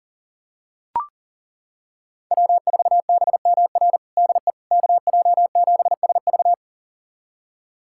Morse code, heard as a keyed tone, about 40 wpm, 700 Hz. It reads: W4BMR DE KJ7SV
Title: USA Callsign DE Callsign - 5 Characters - 40wpm